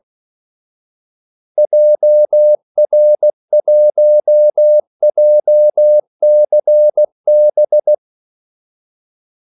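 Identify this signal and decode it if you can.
Morse code, heard as a keyed tone, about 16 wpm, 600 Hz. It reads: JR1JCB
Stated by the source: Morse call sign practice at 16 wpm.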